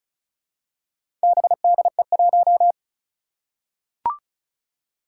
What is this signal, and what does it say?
Morse code, heard as a keyed tone, about 36 wpm, 700 Hz. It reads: BDE1
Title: Four Challenging Letter-Number Combinations 35wpm